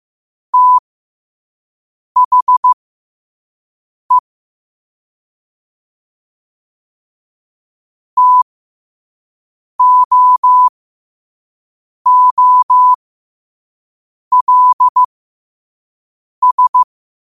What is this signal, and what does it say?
Morse code, heard as a keyed tone, about 15 wpm, spaced as for 5 wpm, 1000 Hz. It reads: THE TOOLS
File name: qotdfiles / 5wpm